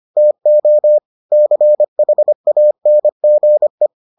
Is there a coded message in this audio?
Yes: Morse code from 0.2 to 3.9 s, heard as a keyed tone, about 25 wpm, 600 Hz. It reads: TO CHANGE